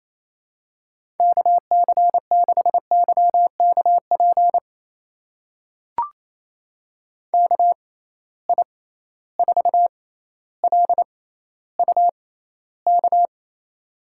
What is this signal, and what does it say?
Morse code, heard as a keyed tone, about 27 wpm, spaced as for 14 wpm, 700 Hz. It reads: KC6YKP KI4LUK
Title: USA Call Signs - 6 Characters - 6x Character Spacing - 28wpm